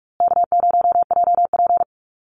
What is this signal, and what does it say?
Morse code, heard as a keyed tone, about 44 wpm, 700 Hz. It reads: K0JP
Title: Call Signs Repeated 2x - 45wpm